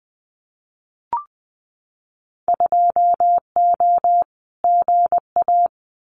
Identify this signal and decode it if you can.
Morse code, heard as a keyed tone, about 20 wpm, 700 Hz. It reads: 2O GA
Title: Winter Field Day Exchanges Encore 20wpm